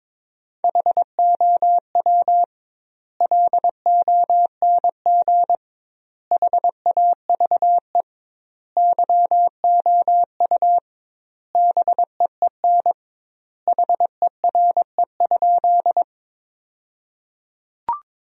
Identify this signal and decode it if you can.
Morse code, heard as a keyed tone, about 22 wpm, 700 Hz. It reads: HOW LONG HAVE YOU BEEN HERE?
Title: 2x Word-Spacing - Sentences from Top 200 Words - 22wpm